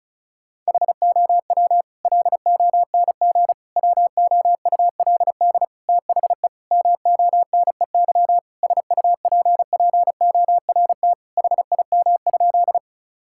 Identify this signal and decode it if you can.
Morse code, heard as a keyed tone, about 36 wpm, 700 Hz. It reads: HOW LONG WOULD THE MONEY SUPPORT HIM?
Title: Sentences from Top 1000 Words - Review Part 5 - 35wpm